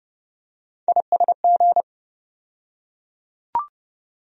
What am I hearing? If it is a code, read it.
Morse code, heard as a keyed tone, about 30 wpm, 700 Hz. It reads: ISG
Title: USA Call Sign Postfixes - 3 Letters 30wpm